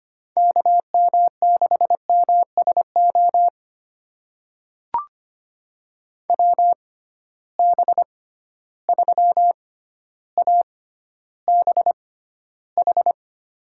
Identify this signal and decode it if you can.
Morse code, heard as a keyed tone, about 25 wpm, spaced as for 13 wpm, 700 Hz. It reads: KM6MSO WB3ABH
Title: USA Call Signs - 6 Characters - 6x Character Spacing - 25wpm